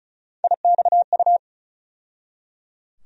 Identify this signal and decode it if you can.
Morse code, heard as a keyed tone, about 34 wpm, 700 Hz. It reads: IXU
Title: USA Call Sign Postfixes - 3 Letters 35wpm